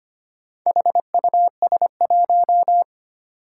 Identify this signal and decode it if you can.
Morse code, heard as a keyed tone, about 25 wpm, 700 Hz. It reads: HUS1
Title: Four Challenging Letter-Number Combinations 25wpm